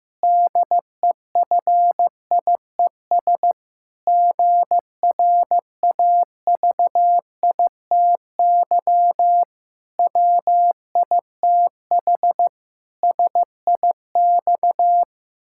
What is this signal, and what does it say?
Morse code, heard as a keyed tone, about 15 wpm, 700 Hz. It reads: DEFIES GRAVITY WITH SIX
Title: Pangram Challenge v2 - 15wpm